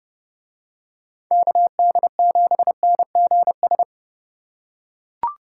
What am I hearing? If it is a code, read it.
Morse code, heard as a keyed tone, about 30 wpm, 700 Hz. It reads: KD7NGS